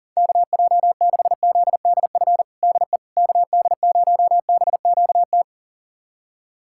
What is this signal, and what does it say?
Morse code, heard as a keyed tone, about 40 wpm, 700 Hz. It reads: KJ6ZDF DE KD0BQT
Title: USA Callsign DE Callsign - 6 Characters - 40wpm